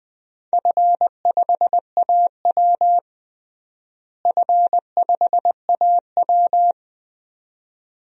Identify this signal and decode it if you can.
Morse code, heard as a keyed tone, about 20 wpm, 700 Hz. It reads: F5AW F5AW
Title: DXCC-100 Call Signs - 4 Characters - Repeated 2x - 20wpm